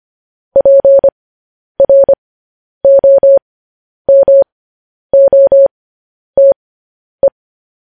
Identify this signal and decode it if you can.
Morse code, heard as a keyed tone, about 25 wpm, spaced as for 10 wpm, 550 Hz. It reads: PROMOTE
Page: News Headlines in Morse Code at 10 WPM